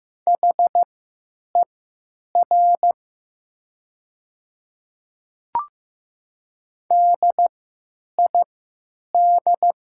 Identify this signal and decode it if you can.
Morse code, heard as a keyed tone, about 15 wpm, spaced as for 9 wpm, 690 Hz. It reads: HER DID
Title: 3x Character Spacing - Top 100 Words - 15wpm